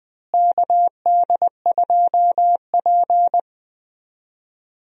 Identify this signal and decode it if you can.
Morse code, heard as a keyed tone, about 20 wpm, 700 Hz. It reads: KD2P
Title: USA Call Signs - 4 Characters - Repeated 3x - 20wpm